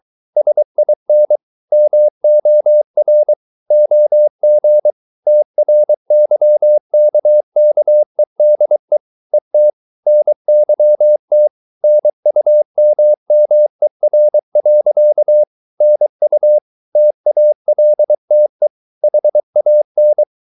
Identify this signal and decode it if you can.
Morse code, heard as a keyed tone, about 23 wpm, 600 Hz. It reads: SIN MOR OG TRYKKEDE ET NYT NUMMER. NU TALTE HAN
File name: Ferie QRS 23wpm